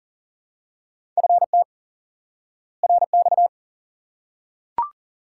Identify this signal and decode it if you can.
Morse code, heard as a keyed tone, about 40 wpm, 700 Hz. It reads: FT RX
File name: IQE Course 3 - Lesson 89 - YAGI - Two Elements - 40wpm